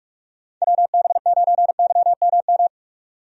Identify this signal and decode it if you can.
Morse code, heard as a keyed tone, about 44 wpm, 700 Hz. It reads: WB9YMM
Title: USA Call Signs - 6 Characters - 45wpm